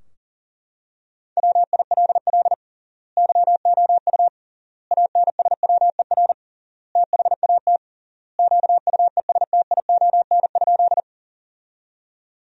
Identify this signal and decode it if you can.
Morse code, heard as a keyed tone, about 40 wpm, 700 Hz. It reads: WILL YOU ANSWER THAT QUESTION?
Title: 3x Word-Spacing - Sentences from Top 400 Words - 40wpm